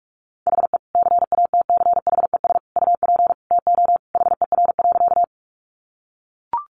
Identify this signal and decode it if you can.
Morse code, heard as a keyed tone, about 44 wpm, 700 Hz. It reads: HE CATCHES UP TO HER.
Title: Additional Easy Phrases 45wpm